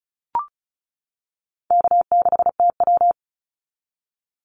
Morse code, heard as a keyed tone, about 36 wpm, 700 Hz. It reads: K6TW
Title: NA SOTA Activator Callsigns 35wpm